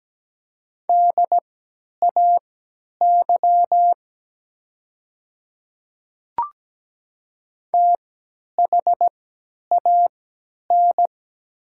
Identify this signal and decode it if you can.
Morse code, heard as a keyed tone, about 17 wpm, spaced as for 10 wpm, 700 Hz. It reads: DAY THAN
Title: 3x Character Spacing - Top 100 Words - 17wpm